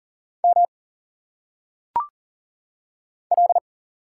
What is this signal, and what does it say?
Morse code, heard as a keyed tone, about 38 wpm, 700 Hz. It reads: M L
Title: Single Letters - Phonetic Pronunciation 40wpm